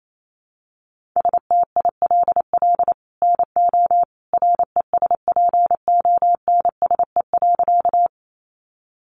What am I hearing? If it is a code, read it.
Morse code, heard as a keyed tone, about 28 wpm, 700 Hz. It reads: STILL NO RESPONSE.
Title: Sentences from Top 1400 Words 28wpm